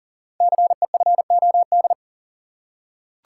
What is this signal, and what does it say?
Morse code, heard as a keyed tone, about 38 wpm, 700 Hz. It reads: CEFOD
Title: Code Groups - 5 Letters - 40wpm